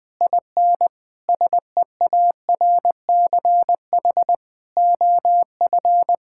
Morse code, heard as a keyed tone, about 21 wpm, 700 Hz. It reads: IN SEARCH OF
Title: Sentences with 7-Word Phrases - 20wpm